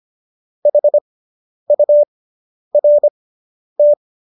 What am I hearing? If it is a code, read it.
Morse code, heard as a keyed tone, about 25 wpm, spaced as for 10 wpm, 590 Hz. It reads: HURT